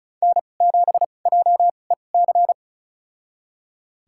Morse code, heard as a keyed tone, about 34 wpm, spaced as for 25 wpm, 700 Hz. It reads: N7JEC
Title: USA Call Signs - 5 Characters - 2x Character Spacing - 35wpm